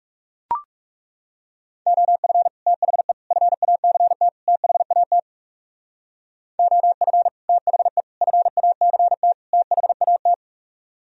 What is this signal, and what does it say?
Morse code, heard as a keyed tone, about 40 wpm, 700 Hz. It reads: OF THE FACT THAT OF THE FACT THAT